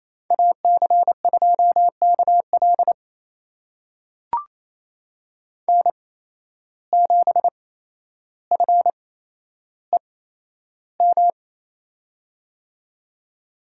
Morse code, heard as a keyed tone, about 28 wpm, spaced as for 12 wpm, 700 Hz. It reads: AC2KL N7FEM
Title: USA Call Signs - 5 Characters - 8x Character Spacing - 28wpm